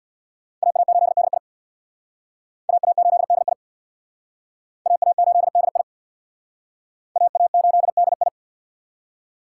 Morse code, heard as a keyed tone, about 62 wpm, 690 Hz. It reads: AA8DI AA8DI AA8DI AA8DI
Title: Speed Racing - USA 2x2 Call Signs 50wpm